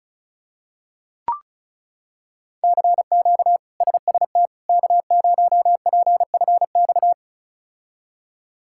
Morse code, heard as a keyed tone, about 36 wpm, 700 Hz. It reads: CQ SST K0PFX
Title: K1USN SST QSO Practice - 35wpm